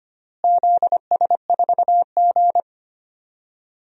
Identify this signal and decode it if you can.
Morse code, heard as a keyed tone, about 25 wpm, 700 Hz. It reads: ZS4G
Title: DXCC-100 Call Signs - 4 Characters - Repeated 2x - 25wpm